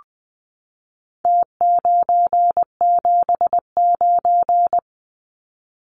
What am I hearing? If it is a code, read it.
Morse code, heard as a keyed tone, about 20 wpm, 700 Hz. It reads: T979
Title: Numbers - Contest Style - 4-Digit 20wpm